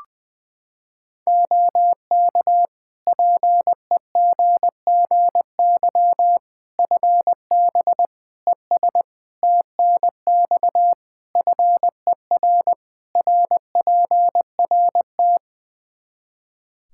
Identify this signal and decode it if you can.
Morse code, heard as a keyed tone, about 21 wpm, 700 Hz. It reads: OK PEGGY FB ES TNX FER RPRT
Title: Practice QSO - Exchange 2 - Info - 20wpm